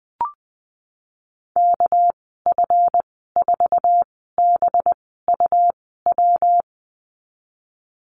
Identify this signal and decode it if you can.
Morse code, heard as a keyed tone, about 20 wpm, spaced as for 15 wpm, 700 Hz. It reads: KF4BUW
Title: 219 Lesson 35g - Number 0 - Call Signs 20wpm